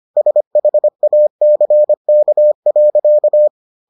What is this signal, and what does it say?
Morse code, heard as a keyed tone, about 25 wpm, 600 Hz. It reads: SHACK.